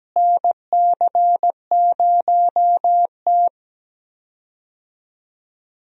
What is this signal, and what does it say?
Morse code, heard as a keyed tone, about 17 wpm, 700 Hz. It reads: NC0T
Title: USA Call Signs - 4 Characters - Repeated 3x - 17wpm